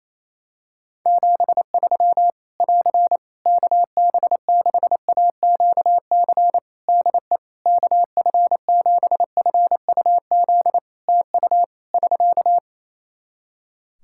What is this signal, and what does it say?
Morse code, heard as a keyed tone, about 28 wpm, 700 Hz. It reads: 73 <AR> KB6AQC DE KF7FUZ TU <SK>
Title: Practice QSO - Exchange 4 - Ending - 28wpm